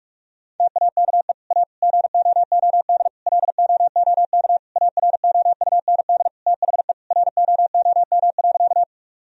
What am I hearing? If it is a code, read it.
Morse code, heard as a keyed tone, about 42 wpm, 700 Hz. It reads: TAKE A GOOD LOOK AROUND THE ROOM.
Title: Sentences from Top 1000 Words - Review Part 3 - 45wpm